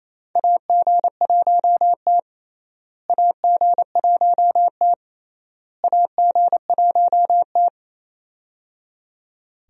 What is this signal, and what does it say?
Morse code, heard as a keyed tone, about 28 wpm, 700 Hz. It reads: AG1T AG1T AG1T
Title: USA Call Signs - 4 Characters - Repeated 3x - 28wpm